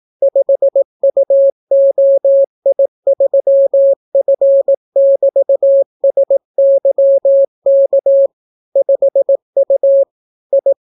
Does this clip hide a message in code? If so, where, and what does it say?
Yes: Morse code from 0.2 to 10.7 s, heard as a keyed tone, about 18 wpm, 550 Hz. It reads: 5UOI3F<BT>SYK 5U I